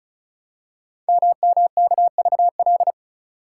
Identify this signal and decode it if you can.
Morse code, heard as a keyed tone, about 36 wpm, 690 Hz. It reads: MMKVL
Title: Code Groups - 5 Letters - 35wpm